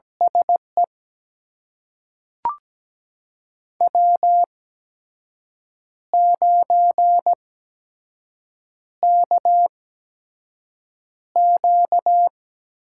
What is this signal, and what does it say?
Morse code, heard as a keyed tone, about 17 wpm, spaced as for 6 wpm, 700 Hz. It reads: SE W9KQ
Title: USA Call Signs - 8x Character Spacing - 17wpm